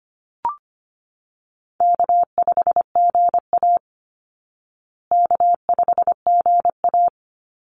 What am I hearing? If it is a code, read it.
Morse code, heard as a keyed tone, about 25 wpm, 700 Hz. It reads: K5GA K5GA
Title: Easy Call Signs Repeated 2x - 25wpm